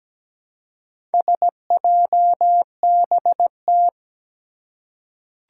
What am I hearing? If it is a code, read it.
Morse code, heard as a keyed tone, about 17 wpm, 700 Hz. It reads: SJBT
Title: Four Challenging Letter-Number Combinations 17wpm